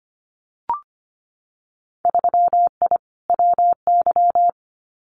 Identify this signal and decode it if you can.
Morse code, heard as a keyed tone, about 25 wpm, 700 Hz. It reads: 3I WY
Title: Winter Field Day Exchanges Encore 25wpm